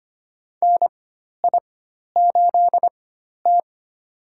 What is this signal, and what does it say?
Morse code, heard as a keyed tone, about 25 wpm, spaced as for 12 wpm, 690 Hz. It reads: NI8T
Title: USA Call Signs - 4 Characters - 4x Character Spacing - 25wpm